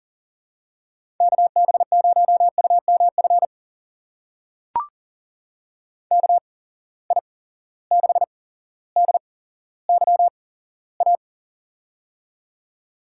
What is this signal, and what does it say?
Morse code, heard as a keyed tone, about 40 wpm, spaced as for 17 wpm, 700 Hz. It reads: KB0UMF KI6DYA
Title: USA Call Signs - 6 Characters - 8x Character Spacing - 40wpm